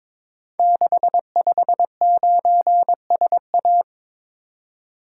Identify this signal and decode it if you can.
Morse code, heard as a keyed tone, about 22 wpm, 700 Hz. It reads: 659SA
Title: Code Groups - 5 Letter:Numbers - 22wpm